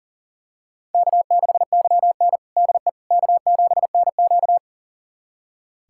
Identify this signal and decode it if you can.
Morse code, heard as a keyed tone, about 40 wpm, 700 Hz. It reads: K6YN DE K7NQ
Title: USA Callsign DE Callsign - 4 Characters - 40wpm